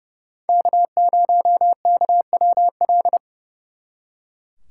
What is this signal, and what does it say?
Morse code, heard as a keyed tone, about 39 wpm, 700 Hz. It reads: K0KWL